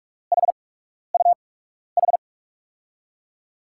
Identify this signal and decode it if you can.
Morse code, heard as a keyed tone, about 42 wpm, spaced as for 12 wpm, 700 Hz. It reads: HUH
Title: USA Call Signs - 6 Characters - 8x Character Spacing - 45wpm